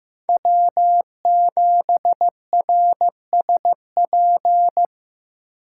Morse code, heard as a keyed tone, about 18 wpm, 700 Hz. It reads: W7RSP
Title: Speed Racing - USA 1x3 Call Signs 15wpm